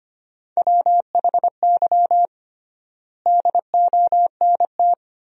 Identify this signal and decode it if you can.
Morse code, heard as a keyed tone, about 25 wpm, 700 Hz. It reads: WHY DONT
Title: 3x Word-Spacing - Sentences from Top 300 Words - 25wpm